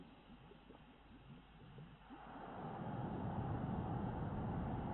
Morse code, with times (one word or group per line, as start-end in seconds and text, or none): none